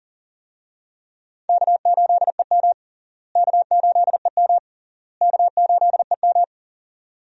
Morse code, heard as a keyed tone, about 40 wpm, 700 Hz. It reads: K8EM K8EM K8EM
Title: USA Call Signs - 4 Characters - Repeated 3x - 40wpm